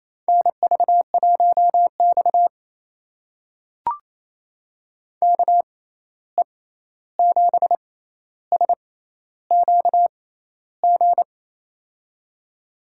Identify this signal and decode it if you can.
Morse code, heard as a keyed tone, about 28 wpm, spaced as for 13 wpm, 700 Hz. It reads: NV1X KE7SQG
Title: USA Call Signs - 6x Character Spacing - 28wpm